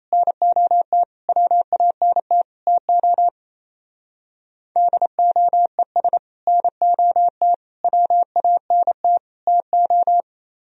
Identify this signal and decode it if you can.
Morse code, heard as a keyed tone, about 34 wpm, 700 Hz. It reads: NOT WANT TO DOES NOT WANT TO
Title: Speed-Racing - Sets of 4 Words 28wpm